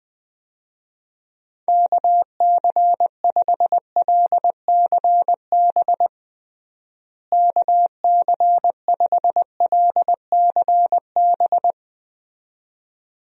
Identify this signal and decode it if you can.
Morse code, heard as a keyed tone, about 21 wpm, 700 Hz. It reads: KC5LCB KC5LCB
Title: USA Call Signs - 6 Characters - Repeated 2x - 20wpm